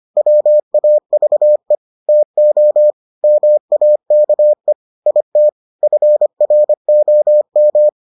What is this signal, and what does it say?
Morse code, heard as a keyed tone, about 25 wpm, 600 Hz. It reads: WAVE TO MAKE IT FROM